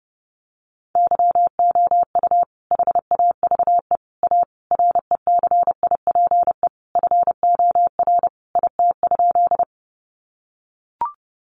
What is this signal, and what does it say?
Morse code, heard as a keyed tone, about 30 wpm, 700 Hz. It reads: YOU HAVE A RECIPE FOR IT?
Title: Sentences from Top 1200 Words 30wpm